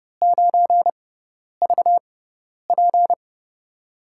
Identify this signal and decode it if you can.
Morse code, heard as a keyed tone, about 30 wpm, spaced as for 10 wpm, 700 Hz. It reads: 9VP
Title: USA Call Signs - 6x Character Spacing - 30wpm